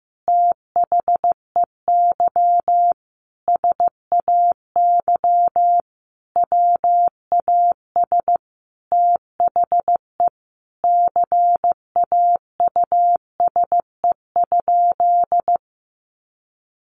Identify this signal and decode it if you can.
Morse code, heard as a keyed tone, about 15 wpm, 700 Hz. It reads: THEY SAY WAS THE CAUSE?
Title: Sentences from Top 200 Words 15wpm